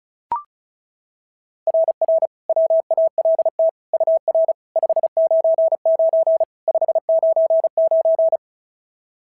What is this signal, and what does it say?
Morse code, heard as a keyed tone, about 36 wpm, 690 Hz. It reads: RR WALT UR 599 599
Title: SOTA QSO Practice 35wpm